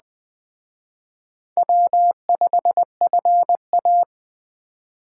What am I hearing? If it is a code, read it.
Morse code, heard as a keyed tone, about 18 wpm, 700 Hz. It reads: W5FA
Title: Speed Racing - USA 1x2 Call Signs 15wpm